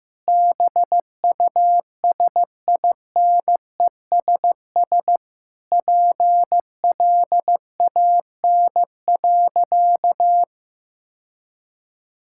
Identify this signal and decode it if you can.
Morse code, heard as a keyed tone, about 15 wpm, 700 Hz. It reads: BUSINESS PLAN.
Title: Sentences from Top 1000 Words - Review Part 6 - 15wpm